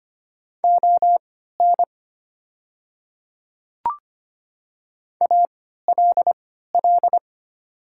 Morse code, heard as a keyed tone, about 25 wpm, spaced as for 14 wpm, 700 Hz. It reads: ON ALL
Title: 3x Character Spacing - Top 100 Words - 25wpm